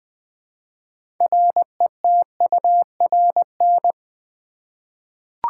Morse code, heard as a keyed tone, about 20 wpm, 700 Hz. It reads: RETURN
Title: Six Letter Words - 20wpm